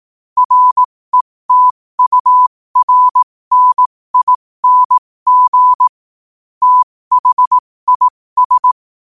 Morse code, heard as a keyed tone, about 18 wpm, spaced as for 15 wpm, 1000 Hz. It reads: RETURNING THIS
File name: qotdfiles / 15wpm